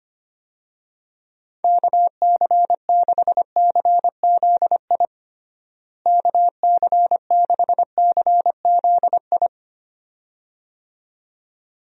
Morse code, heard as a keyed tone, about 25 wpm, 700 Hz. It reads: KC6CZI KC6CZI
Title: USA Call Signs - 6 Characters - Repeated 2x - 25wpm